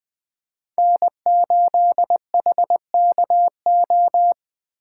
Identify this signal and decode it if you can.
Morse code, heard as a keyed tone, about 20 wpm, 700 Hz. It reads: N8HKO